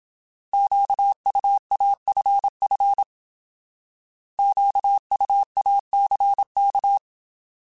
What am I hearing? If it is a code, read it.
Morse code, heard as a keyed tone, about 27 wpm, 800 Hz. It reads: QUAFF QUACK